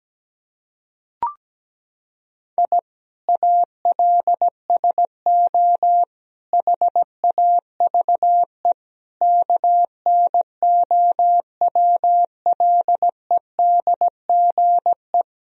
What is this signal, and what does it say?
Morse code, heard as a keyed tone, about 17 wpm, 700 Hz. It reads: I ALSO HAVE KNOWLEDGE
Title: Sentences from Top 1000 Words - Review Part 3 - 17wpm